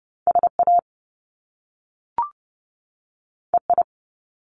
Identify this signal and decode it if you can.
Morse code, heard as a keyed tone, about 30 wpm, 700 Hz. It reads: SA EI
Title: Top Two Letter Combinations in Words 30wpm